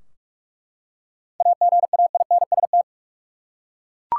Morse code, heard as a keyed tone, about 48 wpm, 700 Hz. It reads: AGAINST
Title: Seven Letter Words - 45wpm